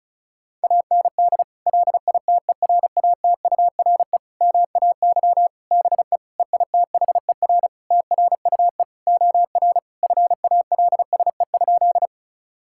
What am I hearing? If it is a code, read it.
Morse code, heard as a keyed tone, about 34 wpm, 700 Hz. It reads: AND LITERATURE MAY BE EITHER TRUE OR FALSE?
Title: Sentences from Top 1600 Words 35wpm